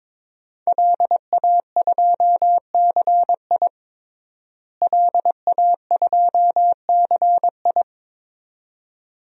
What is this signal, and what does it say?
Morse code, heard as a keyed tone, about 22 wpm, 700 Hz. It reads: LA2CI LA2CI
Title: DXCC-100 Call Signs - 5 Characters - Repeated 2x - 22wpm